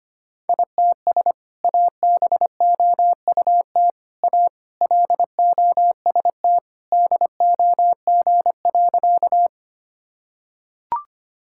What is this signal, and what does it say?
Morse code, heard as a keyed tone, about 25 wpm, 700 Hz. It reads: ITS ABOUT A LOST DOG.
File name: practice / 25wpm / Sentences with 2-Word Phrases - 25wpm